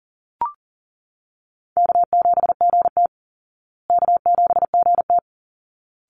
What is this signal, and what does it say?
Morse code, heard as a keyed tone, about 40 wpm, 700 Hz. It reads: K7GT K7GT
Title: Easy Call Signs Repeated 2x - 40wpm